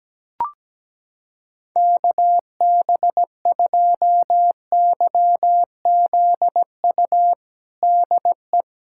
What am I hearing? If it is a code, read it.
Morse code, heard as a keyed tone, about 17 wpm, 700 Hz. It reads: KB2YZU DE
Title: USA Callsign DE Callsign 17wpm